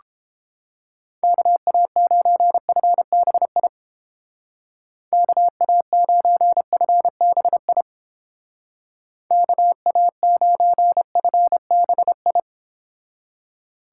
Morse code, heard as a keyed tone, about 29 wpm, 700 Hz. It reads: KA9FBI KA9FBI KA9FBI